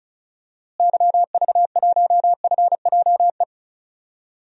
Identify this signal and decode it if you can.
Morse code, heard as a keyed tone, about 34 wpm, 700 Hz. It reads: YV1FJE
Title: DXCC-100 Call Signs - 6 Characters - Repeated 3x - 35wpm